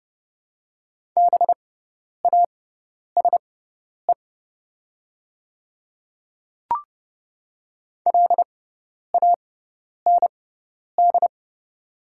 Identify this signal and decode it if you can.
Morse code, heard as a keyed tone, about 31 wpm, spaced as for 10 wpm, 700 Hz. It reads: BASE LAND
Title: 6x Character Spacing - Top 300 Words - 30wpm